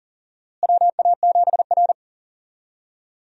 Morse code, heard as a keyed tone, about 40 wpm, 700 Hz. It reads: WA7R
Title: USA Call Signs - 4 Characters - Repeated 2x - 40wpm